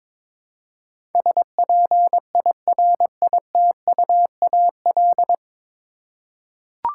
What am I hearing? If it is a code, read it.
Morse code, heard as a keyed tone, about 22 wpm, 700 Hz. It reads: SPIRITUAL
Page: Morse Code Ninja practice 22wpm